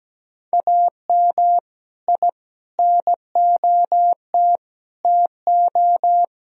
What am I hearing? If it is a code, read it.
Morse code, heard as a keyed tone, about 17 wpm, 700 Hz. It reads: AM I NOT TO
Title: Sentences from Top 1000 Words - Review Part 2 - 17wpm